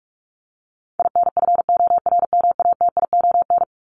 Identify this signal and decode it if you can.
Morse code, heard as a keyed tone, about 44 wpm, 700 Hz. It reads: INFORMATION
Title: Top 501-750 Words 45wpm